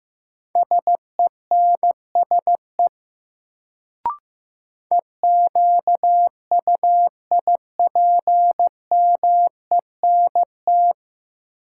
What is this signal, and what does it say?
Morse code, heard as a keyed tone, about 15 wpm, 700 Hz. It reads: SENSE EQUIPMENT
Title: Sentences from Top 700 Words - New Word Drill - 15wpm